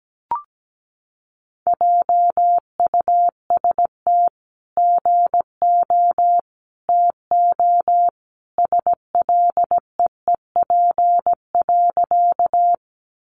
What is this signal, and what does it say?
Morse code, heard as a keyed tone, about 17 wpm, 700 Hz. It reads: JUST GO TO SLEEP.
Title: Sentences from Top 500 Words 17wpm